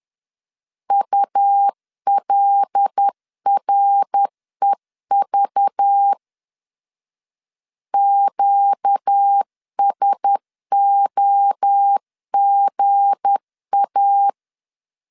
Morse code, heard as a keyed tone, about 10 wpm, 800 Hz. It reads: ULREV QSOGA
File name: LEZIONI / 7 GF9Z / 7 G 10wpm 2501